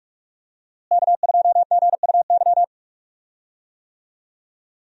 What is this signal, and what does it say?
Morse code, heard as a keyed tone, about 42 wpm, 700 Hz. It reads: K2GUY
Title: USA Call Signs - 5 Characters - Repeated 3x - 45wpm